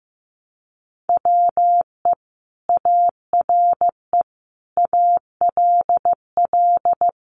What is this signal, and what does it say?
Morse code, heard as a keyed tone, about 15 wpm, 700 Hz. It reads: WE ARE ALL
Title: Sentences from Top 1400 Words 15wpm